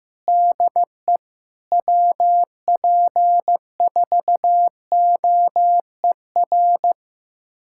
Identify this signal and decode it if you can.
Morse code, heard as a keyed tone, about 15 wpm, 700 Hz. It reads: DE WP4OER
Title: USA Callsign DE Callsign - 6 Characters - 15wpm